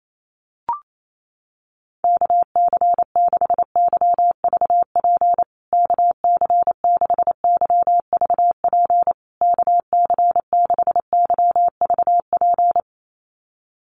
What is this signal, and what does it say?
Morse code, heard as a keyed tone, about 28 wpm, 700 Hz. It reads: KC6YVP KC6YVP KC6YVP
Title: USA Call Signs Repeated 3x 28wpm